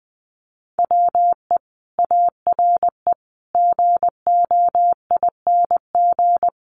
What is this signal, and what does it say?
Morse code, heard as a keyed tone, about 20 wpm, 700 Hz. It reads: WE ARE GOING
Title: Sets of 5 Words Encore 20wpm